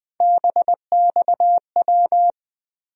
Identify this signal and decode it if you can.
Morse code, heard as a keyed tone, about 20 wpm, 700 Hz. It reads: BXW
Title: USA Call Sign Postfixes - 3 Letters 20wpm